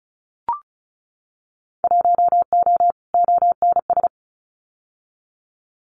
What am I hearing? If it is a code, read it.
Morse code, heard as a keyed tone, about 36 wpm, 700 Hz. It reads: 1O ONS
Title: Winter Field Day Exchanges Encore 35wpm